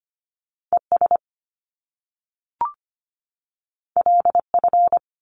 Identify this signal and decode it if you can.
Morse code, heard as a keyed tone, about 25 wpm, 700 Hz. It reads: ES LF
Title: Top Two Letter Combinations in Words 25wpm